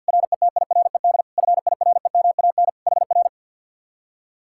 Morse code, heard as a keyed tone, about 61 wpm, 700 Hz. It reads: RETIRED FIREMAN HR
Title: Speed-Racing Common QSO Phrases Encore - 50wpm